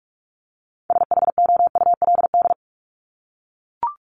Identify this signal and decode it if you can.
Morse code, heard as a keyed tone, about 44 wpm, 700 Hz. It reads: SHOULD
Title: Top 500 Words Repeated 45wpm